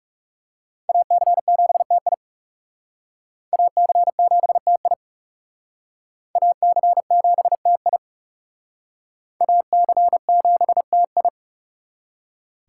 Morse code, heard as a keyed tone, about 35 wpm, 700 Hz. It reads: AC7TI AC7TI AC7TI AC7TI
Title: Speed Racing - USA Call Signs 30wpm